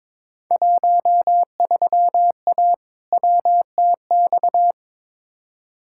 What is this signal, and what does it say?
Morse code, heard as a keyed tone, about 22 wpm, 700 Hz. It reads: 13A WTX
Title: Field Day Exchanges Encore - 22wpm